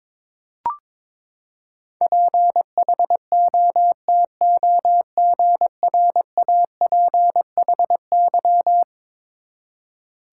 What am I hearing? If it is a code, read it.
Morse code, heard as a keyed tone, about 22 wpm, 700 Hz. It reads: PHOTOGRAPHY
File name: practice / 22wpm / Eleven Letter Words - 22wpm